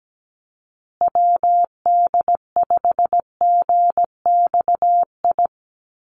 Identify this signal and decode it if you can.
Morse code, heard as a keyed tone, about 17 wpm, 700 Hz. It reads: WD5GXI
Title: USA Call Signs Repeated 2x 17wpm